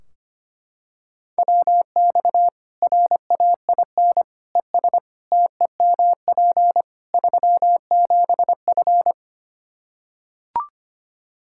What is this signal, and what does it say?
Morse code, heard as a keyed tone, about 25 wpm, 700 Hz. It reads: WX RAIN ES TEMP 37F
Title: Practice QSO - Exchange 2 - Info - 25wpm